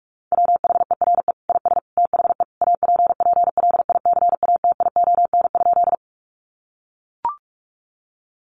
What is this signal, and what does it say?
Morse code, heard as a keyed tone, about 44 wpm, 700 Hz. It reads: WHERE IS THE APPLICATION?
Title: Sentences from Top 1200 Words 45wpm